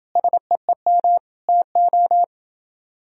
Morse code, heard as a keyed tone, about 30 wpm, 700 Hz. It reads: SEEM TO